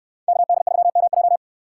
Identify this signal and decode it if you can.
Morse code, heard as a keyed tone, about 64 wpm, 700 Hz. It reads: KD3MJ